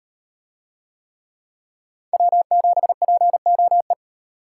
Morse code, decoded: W7POE

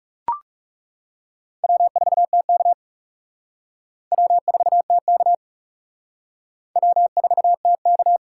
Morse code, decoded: W4TK W4TK W4TK

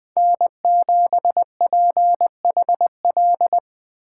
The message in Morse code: N7PHL

